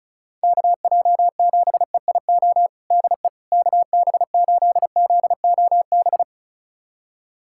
Morse code, heard as a keyed tone, about 34 wpm, 690 Hz. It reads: KJ7EIO DE KB8ZOB
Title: USA Callsign DE Callsign 35wpm